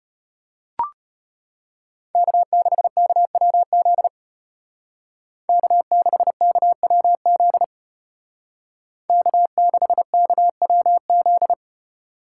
Morse code, heard as a keyed tone, about 33 wpm, 700 Hz. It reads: K6KWZ K6KWZ K6KWZ